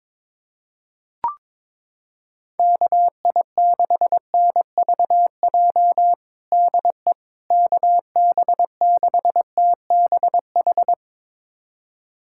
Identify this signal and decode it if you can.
Morse code, heard as a keyed tone, about 22 wpm, 700 Hz. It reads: KI6NVJ DE KB6TBH
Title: USA Callsign DE Callsign - 6 Characters - 22wpm